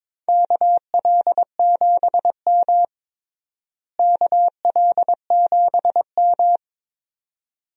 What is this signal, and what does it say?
Morse code, heard as a keyed tone, about 22 wpm, 700 Hz. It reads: KL7M KL7M
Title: DXCC-100 Call Signs - 4 Characters - Repeated 2x - 22wpm